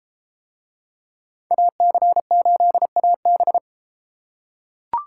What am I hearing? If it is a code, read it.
Morse code, heard as a keyed tone, about 32 wpm, 700 Hz. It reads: AC8AB